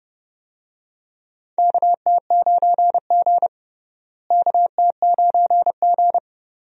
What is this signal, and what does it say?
Morse code, heard as a keyed tone, about 30 wpm, 700 Hz. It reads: KT9G KT9G
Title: USA Call Signs - 4 Characters - Repeated 2x - 30wpm